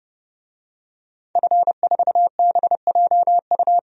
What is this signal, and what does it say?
Morse code, heard as a keyed tone, about 30 wpm, 700 Hz. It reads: F4BJU
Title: DXCC-100 Call Signs - 5 Characters - 30wpm